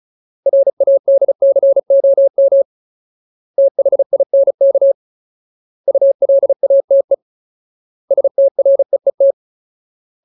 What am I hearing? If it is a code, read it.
Morse code, heard as a keyed tone, about 37 wpm, 550 Hz. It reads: RADCOM THINK ULATE STREET
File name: Ultimate Word List / 35WPM Ultimate